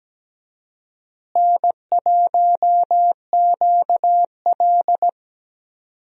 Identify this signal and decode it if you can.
Morse code, heard as a keyed tone, about 17 wpm, 700 Hz. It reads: N1QL
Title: USA Call Signs - 4 Characters - Repeated 2x - 17wpm